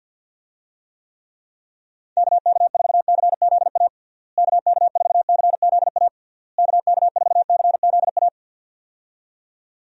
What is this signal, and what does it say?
Morse code, heard as a keyed tone, about 50 wpm, 700 Hz. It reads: KK4CZA KK4CZA KK4CZA
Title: USA Call Signs - 6 Characters - Repeated 3x - 50wpm